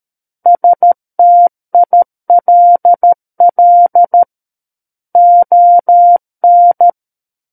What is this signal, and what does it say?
Morse code, heard as a keyed tone, about 13 wpm, 700 Hz. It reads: STILL ON
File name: Book One / 04-Chapter 4 - 13 WPM